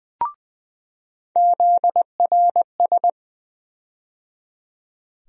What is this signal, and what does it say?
Morse code, heard as a keyed tone, about 20 wpm, 700 Hz. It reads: ZRS
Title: USA Call Sign Postfixes - 3 Letters 20wpm